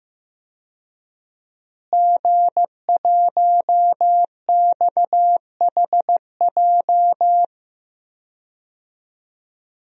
Morse code, heard as a keyed tone, about 15 wpm, 700 Hz. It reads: G1XHJ
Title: DXCC-100 Call Signs - 5 Characters - Repeated 3x - 15wpm